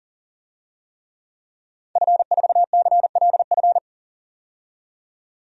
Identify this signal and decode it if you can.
Morse code, heard as a keyed tone, about 40 wpm, 700 Hz. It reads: F4CLF
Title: DXCC-100 Call Signs - 5 Characters - 40wpm